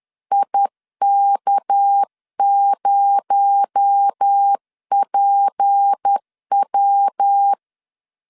Morse code, heard as a keyed tone, about 10 wpm, 800 Hz. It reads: IK0PW
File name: LEZIONI / 6 PKX0 / 6 PKX0 10wpm 2707